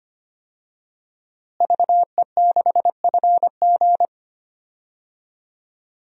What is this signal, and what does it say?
Morse code, heard as a keyed tone, about 25 wpm, 700 Hz. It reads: VE6FG
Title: DXCC-100 Call Signs - 5 Characters - Repeated 3x - 25wpm